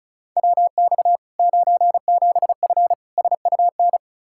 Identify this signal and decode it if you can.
Morse code, heard as a keyed tone, about 36 wpm, 700 Hz. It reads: WX 97F SUN
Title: Practice QSO - Exchange 2 - Info - 35wpm